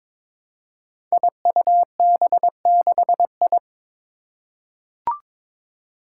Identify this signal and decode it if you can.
Morse code, heard as a keyed tone, about 22 wpm, 700 Hz. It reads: IUB6I
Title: Five Challenging Letter-Number Combinations 22wpm